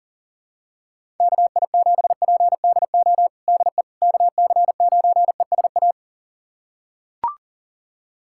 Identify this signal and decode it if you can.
Morse code, heard as a keyed tone, about 37 wpm, 700 Hz. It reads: KI7PDO DE KC9ESA
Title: USA Callsign DE Callsign - 6 Characters - 40wpm